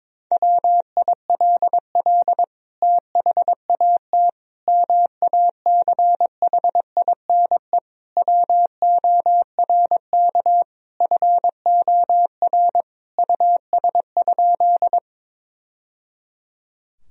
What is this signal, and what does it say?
Morse code, heard as a keyed tone, about 22 wpm, 700 Hz. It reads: WILL THAT MACHINE WORK FOR US?
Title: Sentences from Top 1000 Words - Review Part 2 - 22wpm